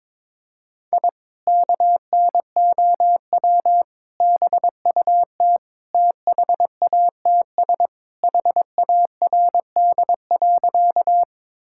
Morse code, heard as a keyed tone, about 22 wpm, 700 Hz. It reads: I KNOW BUT THATS HARD.
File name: practice / 22wpm / Sentences with 2-Word Phrases - 22wpm